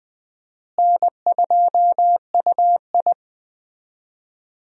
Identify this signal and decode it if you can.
Morse code, heard as a keyed tone, about 20 wpm, 700 Hz. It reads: N2UI
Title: USA Call Signs - 4 Characters - Repeated 3x - 20wpm